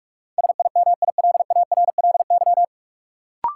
Morse code, heard as a keyed tone, about 47 wpm, 700 Hz. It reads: SIMILARLY
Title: Sentences from Top 1400 Words - New Word Drill - 45wpm